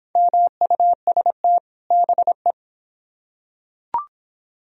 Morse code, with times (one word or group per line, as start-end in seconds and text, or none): 0.2-1.6 MUST
1.9-2.5 BE